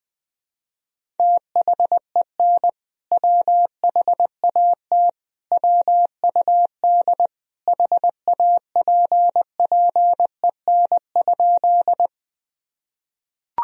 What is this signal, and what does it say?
Morse code, heard as a keyed tone, about 20 wpm, 700 Hz. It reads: THEN WHAT WUD HAPPEN?
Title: Sentences from Top 400 Words and Abbreviations 20wpm